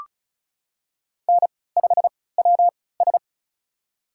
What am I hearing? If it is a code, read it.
Morse code, heard as a keyed tone, about 36 wpm, spaced as for 20 wpm, 700 Hz. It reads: N5WS
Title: USA Call Signs - 4 Characters - 3x Character Spacing - 35wpm